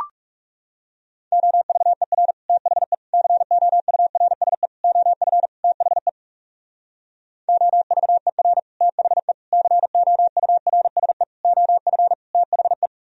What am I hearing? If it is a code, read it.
Morse code, heard as a keyed tone, about 44 wpm, 700 Hz. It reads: OVER THE COURSE OF THE OVER THE COURSE OF THE